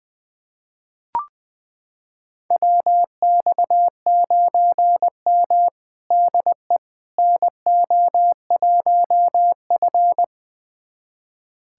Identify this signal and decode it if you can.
Morse code, heard as a keyed tone, about 20 wpm, 700 Hz. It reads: WX9M DE NO1F
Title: USA Callsign DE Callsign - 4 Characters - 20wpm